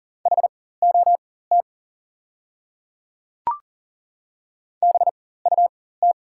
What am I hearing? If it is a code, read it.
Morse code, heard as a keyed tone, about 41 wpm, spaced as for 19 wpm, 690 Hz. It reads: HOT BUT
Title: 4x Character Spacing - Top 100 Words - 40wpm